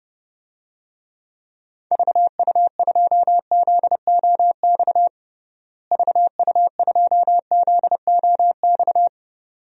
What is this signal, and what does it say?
Morse code, heard as a keyed tone, about 30 wpm, 700 Hz. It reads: VU2ZOX VU2ZOX
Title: DXCC-100 Call Signs - 6 Characters - Repeated 2x - 30wpm